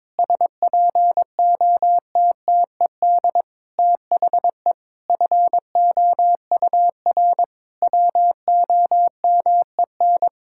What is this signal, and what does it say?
Morse code, heard as a keyed tone, about 22 wpm, 700 Hz. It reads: SPOTTED THE FOUR WOMEN
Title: Pangram Challenge v2 - 22wpm